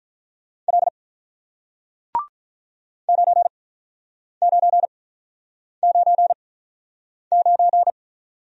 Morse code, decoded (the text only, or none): L 9999